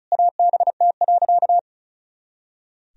Morse code, heard as a keyed tone, about 36 wpm, 700 Hz. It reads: ABT.
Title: Sentences from Top 400 Words and Abbreviations 35wpm